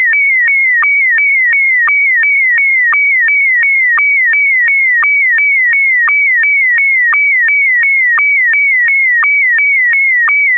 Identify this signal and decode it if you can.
SSTV (slow-scan television), Scottie DX.